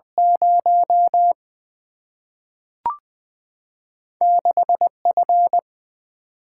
Morse code, heard as a keyed tone, about 21 wpm, 700 Hz. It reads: 0 6 F
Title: Single-Letter Single-Number 20wpm